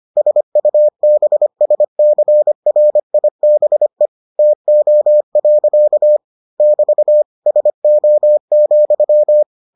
Morse code, heard as a keyed tone, about 25 wpm, 600 Hz. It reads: SUBSCRIBE TO. <BT> SO,